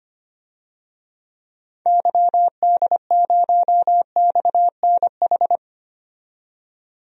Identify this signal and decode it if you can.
Morse code, heard as a keyed tone, about 25 wpm, 700 Hz. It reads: YD0XNH
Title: DXCC-100 Call Signs - 6 Characters - 25wpm